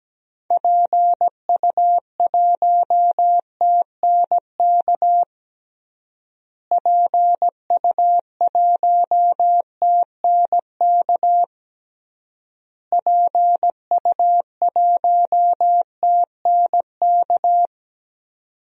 Morse code, heard as a keyed tone, about 17 wpm, 700 Hz. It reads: PU1TNK PU1TNK PU1TNK
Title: DXCC-100 Call Signs - 6 Characters - Repeated 3x - 17wpm